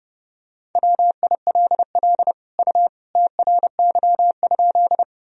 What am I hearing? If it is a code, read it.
Morse code, heard as a keyed tone, about 30 wpm, 700 Hz. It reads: WILL U TRY?
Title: Sentences from Top 200 Words and Abbreviations 30wpm